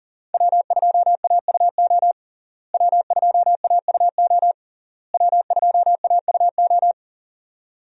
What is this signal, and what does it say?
Morse code, heard as a keyed tone, about 40 wpm, 700 Hz. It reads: W2AUO W2AUO W2AUO
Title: USA Call Signs - 5 Characters - Repeated 3x - 40wpm